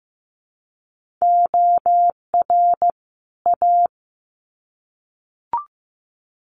Morse code, heard as a keyed tone, about 15 wpm, 700 Hz. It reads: OR A
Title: Sets of 2 Words 15wpm